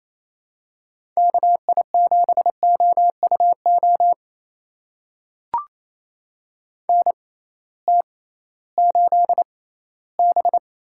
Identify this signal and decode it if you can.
Morse code, heard as a keyed tone, about 28 wpm, spaced as for 16 wpm, 700 Hz. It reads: KI7OUO NT8B
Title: USA Call Signs - 6x Character Spacing - 28wpm